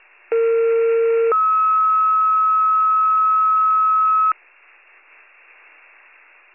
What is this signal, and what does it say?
Two-tone page, heard as a one-second tone, then a three-second tone, about 460 then 1290 Hz. It is heard from 0.3 to 4.3 s.